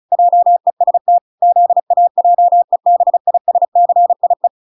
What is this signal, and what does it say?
Morse code, heard as a keyed tone, about 34 wpm, 700 Hz. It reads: JEST ZAJEBISCIE